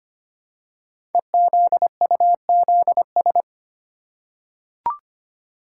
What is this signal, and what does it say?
Morse code, heard as a keyed tone, about 25 wpm, 700 Hz. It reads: EZUZS
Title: Code Groups - 5 Letters - 25wpm